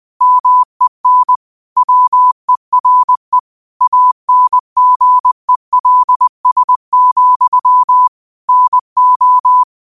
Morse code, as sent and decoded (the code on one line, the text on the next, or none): -- . -. / .-- . .-. / .- -. --. .-.. ... --..-- / -. ---
MEN WERE ANGELS, NO